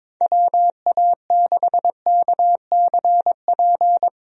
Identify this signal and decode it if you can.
Morse code, heard as a keyed tone, about 22 wpm, 700 Hz. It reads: WA6KCP
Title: USA Call Signs - 6 Characters - 22wpm